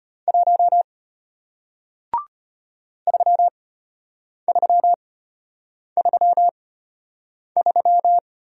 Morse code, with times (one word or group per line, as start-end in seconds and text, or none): 0.3-0.8 1
3.1-8.2 3333